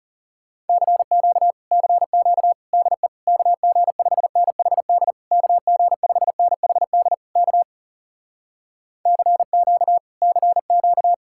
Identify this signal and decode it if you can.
Morse code, heard as a keyed tone, about 42 wpm, 700 Hz. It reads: CQ CQ DE KG5NHD KG5NHD K CQ CQ